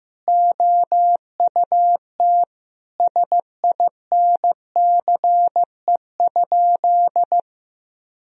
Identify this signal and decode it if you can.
Morse code, heard as a keyed tone, about 15 wpm, 700 Hz. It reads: OUT SINCE?
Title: Sentences from Top 1000 Words - Review Part 1 - 15wpm